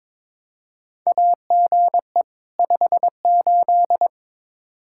Morse code, heard as a keyed tone, about 22 wpm, 700 Hz. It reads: AGE 58
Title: Practice QSO - Exchange 3 - More Info - 22wpm